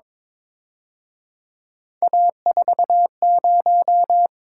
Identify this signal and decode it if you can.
Morse code, heard as a keyed tone, about 26 wpm, 700 Hz. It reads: A40
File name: Speed-Racing - Three Letter-Number Combinations - 22wpm